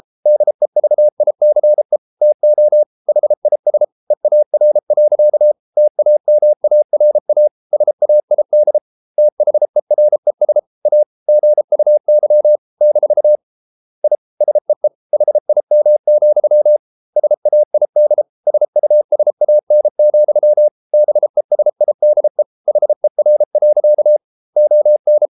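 Morse code, heard as a keyed tone, about 33 wpm, 600 Hz. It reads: DEVICE TO HIS EAR. TAMARA SAID THERES A GUY - I SEE HIM, SAID SUSAN, BESIDE HER. ON
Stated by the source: Morse code practice at 32 wpm from CW med Gnister, reading Ken Follett's novel Never in English.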